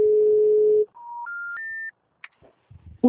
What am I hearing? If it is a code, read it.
Special information tone.
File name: out- -2002- - - .28989